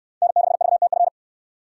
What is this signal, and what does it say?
Morse code, heard as a keyed tone, about 62 wpm, 700 Hz. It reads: N6VTF